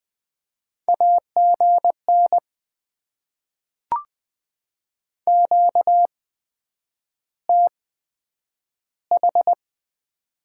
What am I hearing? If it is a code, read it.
Morse code, heard as a keyed tone, about 20 wpm, spaced as for 9 wpm, 700 Hz. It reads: AGN QTH
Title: Top 20 QSO Elements - 8x Character Spacing - 20wpm